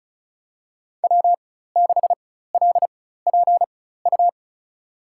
Morse code, heard as a keyed tone, about 36 wpm, spaced as for 16 wpm, 700 Hz. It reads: W6LPU
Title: USA Call Signs - 5 Characters - 4x Character Spacing - 35wpm